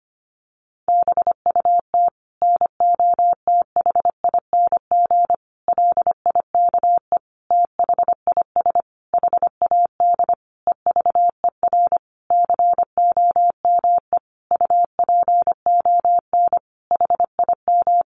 Morse code, heard as a keyed tone, about 25 wpm, 700 Hz. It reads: BUT NOTHING LIKE THIS HAD EVER COME UPON HIM
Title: Sentences from Top 1100 Words 25wpm